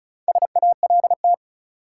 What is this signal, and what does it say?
Morse code, heard as a keyed tone, about 34 wpm, 700 Hz. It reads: SALT
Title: Four Letter Words - 35wpm